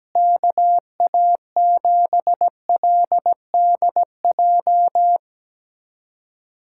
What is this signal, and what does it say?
Morse code, heard as a keyed tone, about 17 wpm, 700 Hz. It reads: KA7LDJ
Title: USA Callsign DE Callsign - 6 Characters - 17wpm